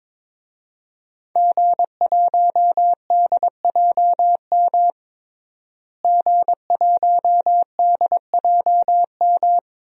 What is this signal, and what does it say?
Morse code, heard as a keyed tone, about 22 wpm, 700 Hz. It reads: G1DJM G1DJM
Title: DXCC-100 Call Signs - 5 Characters - Repeated 2x - 22wpm